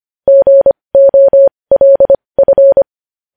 Morse code, heard as a keyed tone, about 25 wpm, spaced as for 20 wpm, 550 Hz. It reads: GOLF
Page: news-headlines-in-morse-code-at-20-wpm